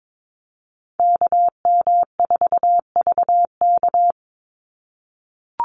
Morse code, heard as a keyed tone, about 22 wpm, 700 Hz. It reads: KM4VK
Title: USA Call Signs 22wpm